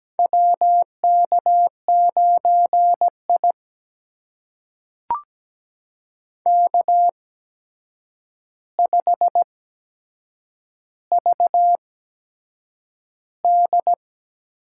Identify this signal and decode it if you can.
Morse code, heard as a keyed tone, about 17 wpm, spaced as for 7 wpm, 700 Hz. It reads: WK9I K5VD
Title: USA Call Signs - 4 Characters - 8x Character Spacing - 17wpm